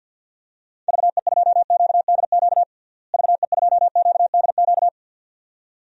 Morse code, heard as a keyed tone, about 47 wpm, 700 Hz. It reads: VE2QDQ VE2QDQ
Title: DXCC-100 Call Signs - 6 Characters - Repeated 2x - 50wpm